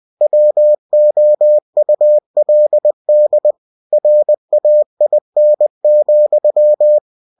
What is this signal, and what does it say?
Morse code, heard as a keyed tone, about 20 wpm, 600 Hz. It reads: WOULD RAIN,
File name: winnie-20wpm